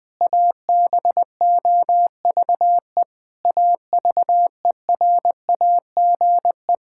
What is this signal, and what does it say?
Morse code, heard as a keyed tone, about 21 wpm, 700 Hz. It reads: ABOVE AVERAGE
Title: Sentences from Top 1000 Words - Review Part 6 - 20wpm